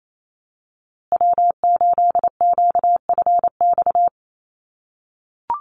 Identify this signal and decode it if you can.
Morse code, heard as a keyed tone, about 28 wpm, 700 Hz. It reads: W8QFX
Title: USA Call Signs Repeated 2x 28wpm